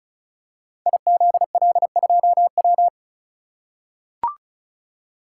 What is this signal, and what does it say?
Morse code, heard as a keyed tone, about 35 wpm, 700 Hz. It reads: IZL2W
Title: Code Groups - 5 Letter:Numbers - 35wpm